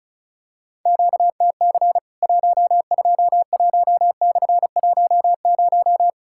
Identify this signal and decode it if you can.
Morse code, heard as a keyed tone, about 36 wpm, 700 Hz. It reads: QTC 121/10
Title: Worked All Europe DX Contest - QTC Exchange - 35wpm